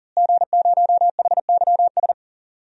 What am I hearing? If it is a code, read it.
Morse code, heard as a keyed tone, about 40 wpm, 700 Hz. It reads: G0HYS
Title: DXCC-100 Call Signs - 5 Characters - Repeated 3x - 40wpm